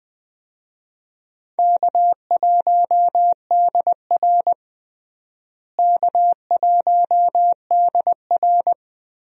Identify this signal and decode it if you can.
Morse code, heard as a keyed tone, about 20 wpm, 700 Hz. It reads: K1DR K1DR
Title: USA Call Signs - 4 Characters - Repeated 2x - 20wpm